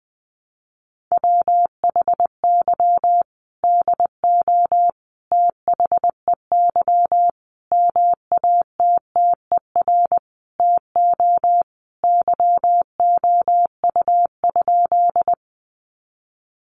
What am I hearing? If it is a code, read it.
Morse code, heard as a keyed tone, about 20 wpm, 700 Hz. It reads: WHY DO THEY MATTER TO YOU?
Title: Sentences from Top 700 Words 20wpm